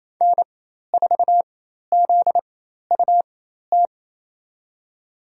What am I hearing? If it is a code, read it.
Morse code, heard as a keyed tone, about 28 wpm, spaced as for 13 wpm, 700 Hz. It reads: N4ZUT
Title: USA Call Signs - 5 Characters - 4x Character Spacing - 28wpm